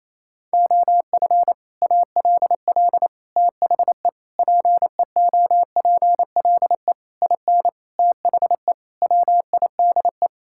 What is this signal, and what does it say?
Morse code, heard as a keyed tone, about 27 wpm, 700 Hz. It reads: OF ALL THE PEOPLE IN THE WIDE
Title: Sentences from Top 1000 Words - Review Part 6 - 28wpm